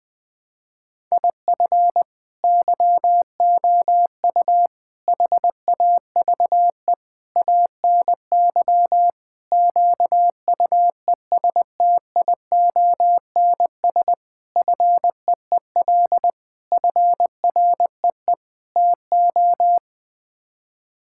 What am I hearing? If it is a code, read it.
Morse code, heard as a keyed tone, about 20 wpm, 700 Hz. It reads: IF YOU HAVE ANY QUESTIONS FEEL FREE TO
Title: Sets of 8 Words - 20wpm